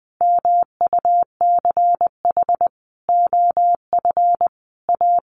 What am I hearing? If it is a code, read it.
Morse code, heard as a keyed tone, about 20 wpm, 700 Hz. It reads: MUCH OF A
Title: Sentences from Top 1400 Words 20wpm